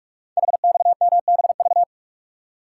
Morse code, heard as a keyed tone, about 44 wpm, 690 Hz. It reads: HXMBV
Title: Code Groups - 5 Letters - 45wpm